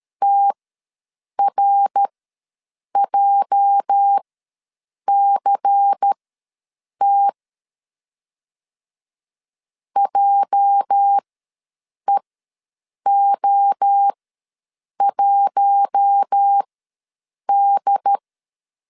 Morse code, heard as a keyed tone, about 13 wpm, spaced as for 7 wpm, 800 Hz. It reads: TRJCT JEO1D
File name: LEZIONI / 8 YJ45 / 8 YJ ws 12wpm 3342